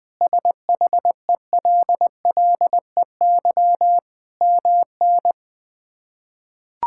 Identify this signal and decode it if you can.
Morse code, heard as a keyed tone, about 20 wpm, 700 Hz. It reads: SHELLEY MN
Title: K1USN SST QSO Practice - 20wpm